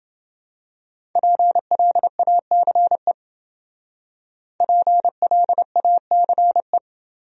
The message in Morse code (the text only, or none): PLACE PLACE